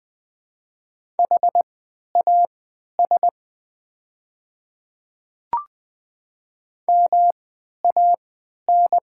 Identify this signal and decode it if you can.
Morse code, heard as a keyed tone, about 20 wpm, spaced as for 11 wpm, 690 Hz. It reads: HAS MAN
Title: 3x Character Spacing - Top 200 Words - 20wpm